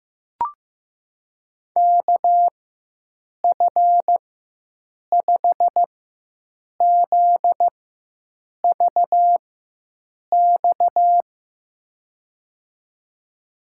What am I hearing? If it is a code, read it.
Morse code, heard as a keyed tone, about 15 wpm, spaced as for 7 wpm, 700 Hz. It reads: KF5ZVX
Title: USA Call Signs - 6 Characters - 4x Character Spacing - 15wpm